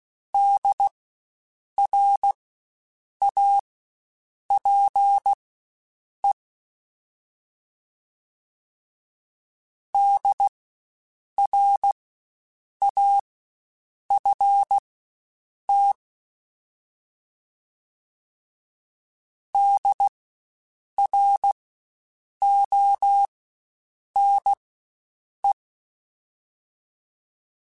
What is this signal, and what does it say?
Morse code, heard as a keyed tone, about 16 wpm, spaced as for 7 wpm, 800 Hz. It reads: DRAPE DRAFT DRONE
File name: Spaced Alphas / Spaced Alphas mp3s / Ds sp @15